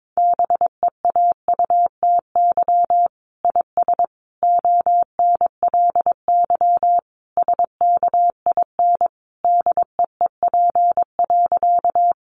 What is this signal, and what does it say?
Morse code, heard as a keyed tone, about 22 wpm, 700 Hz. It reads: BEAUTY IS ONLY SKIN DEEP.
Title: Sentences from Top 1300 Words 22wpm